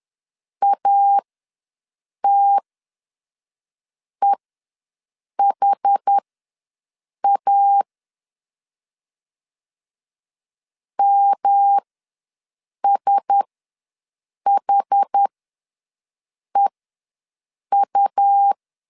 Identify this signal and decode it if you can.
Morse code, heard as a keyed tone, about 10 wpm, spaced as for 6 wpm, 800 Hz. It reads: ATEHA MSHEU
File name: LEZIONI / 5 LBVC / 5 LBVC ws 10wpm 3733